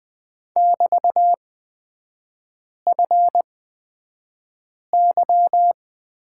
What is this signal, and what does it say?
Morse code, heard as a keyed tone, about 20 wpm, spaced as for 5 wpm, 700 Hz. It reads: <BT>FY
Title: 224 Lesson 36e - BT - Three Characters 20wpm